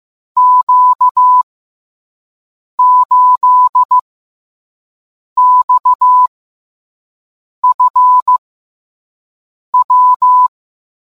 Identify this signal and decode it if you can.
Morse code, heard as a keyed tone, about 15 wpm, 1000 Hz. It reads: Q 8 X F W